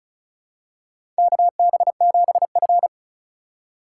Morse code, heard as a keyed tone, about 36 wpm, 690 Hz. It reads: KB7F
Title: USA Call Signs - 3x Character Spacing - 35wpm